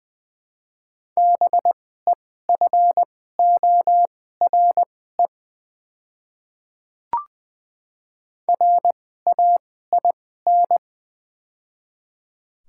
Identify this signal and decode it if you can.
Morse code, heard as a keyed tone, about 20 wpm, spaced as for 14 wpm, 700 Hz. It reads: BEFORE RAIN
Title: 2x Character Spacing - Top 300 Words - 20wpm